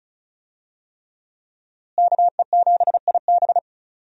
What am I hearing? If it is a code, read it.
Morse code, heard as a keyed tone, about 36 wpm, 700 Hz. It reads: KE7IB
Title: USA Call Signs - 5 Characters - Repeated 3x - 35wpm